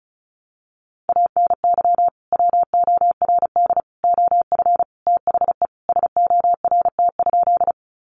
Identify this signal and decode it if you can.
Morse code, heard as a keyed tone, about 34 wpm, 700 Hz. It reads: ANY WORD OF THE SORT?
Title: Sentences from Top 1300 Words 35wpm